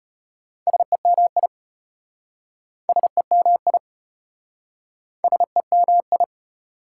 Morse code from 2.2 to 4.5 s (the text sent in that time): SEMI